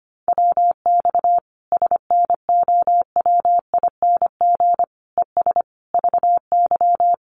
Morse code, heard as a keyed tone, about 25 wpm, 700 Hz. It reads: WX SNOWING ES VY
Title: Common QSO Phrases 25wpm